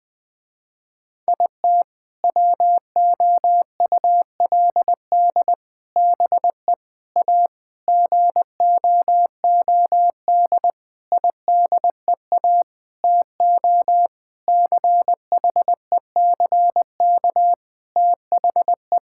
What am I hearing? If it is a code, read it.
Morse code, heard as a keyed tone, about 20 wpm, 700 Hz. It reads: IT WOULD BE A GOOD IDEA TO CHECK THE
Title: Sentences with 6-Word Phrases - Spoken Beginning - 20wpm